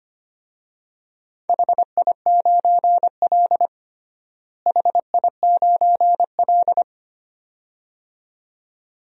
Morse code, heard as a keyed tone, about 25 wpm, 700 Hz. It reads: HI9L HI9L
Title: DXCC-100 Call Signs - 4 Characters - Repeated 2x - 25wpm